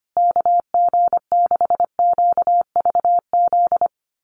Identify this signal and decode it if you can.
Morse code, heard as a keyed tone, about 25 wpm, 700 Hz. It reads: KG6QVZ
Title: USA Call Signs Repeated 3x 25wpm